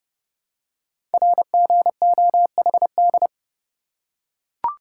Morse code, heard as a keyed tone, about 30 wpm, 700 Hz. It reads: RGOHD
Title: Code Groups - 5 Letters - 30wpm